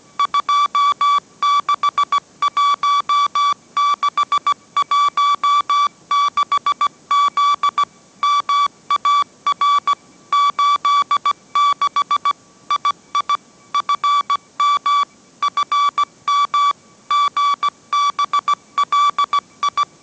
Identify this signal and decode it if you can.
Morse code, heard as a keyed tone, about 17 wpm, 1150 Hz. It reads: 261616Z MAR 86 II FM FM GBLI
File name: mrx / mrx080 18wpm